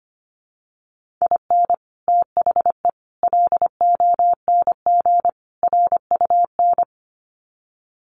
Morse code, heard as a keyed tone, about 25 wpm, 700 Hz. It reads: IN THE LONG RUN
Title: Sets of 4 Words 25wpm